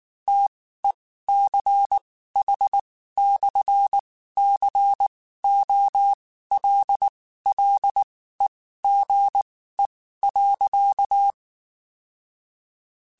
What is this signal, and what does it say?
Morse code, heard as a keyed tone, about 19 wpm, spaced as for 14 wpm, 800 Hz. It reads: TECH/COLLEGE.
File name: (spaced) / QSOs Spaced-mp3s / WD0BGP 1 @18 sp